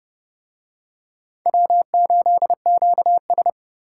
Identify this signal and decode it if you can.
Morse code, heard as a keyed tone, about 30 wpm, 690 Hz. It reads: W8QS